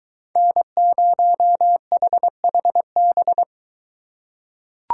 Morse code, heard as a keyed tone, about 21 wpm, 700 Hz. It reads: N0HHB